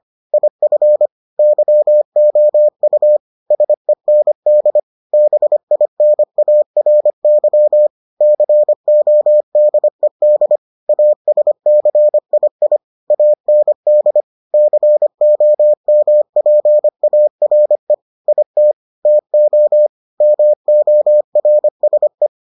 Morse code, heard as a keyed tone, about 25 wpm, 600 Hz. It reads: IF YOU SEND BINARY CODED ASCII AND COMPARE IT TO MORSE